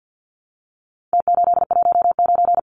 Morse code, heard as a keyed tone, about 50 wpm, 700 Hz. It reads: T819
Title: Numbers - Contest Style - 4-Digit 50wpm